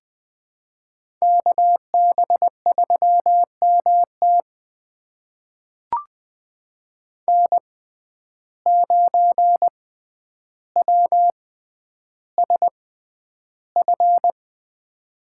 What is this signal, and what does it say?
Morse code, heard as a keyed tone, about 20 wpm, spaced as for 10 wpm, 700 Hz. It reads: KB3MT N9WSF
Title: USA Call Signs - 5 Characters - 6x Character Spacing - 20wpm